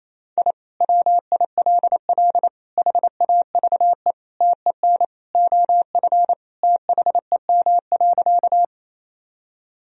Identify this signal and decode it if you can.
Morse code, heard as a keyed tone, about 28 wpm, 700 Hz. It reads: I WILL HAVE TEN OF THEM.
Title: Sentences from Top 1000 Words - Review Part 2 - 28wpm